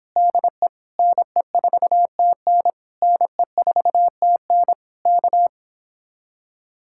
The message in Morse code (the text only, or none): DE NE4TN NE4TN K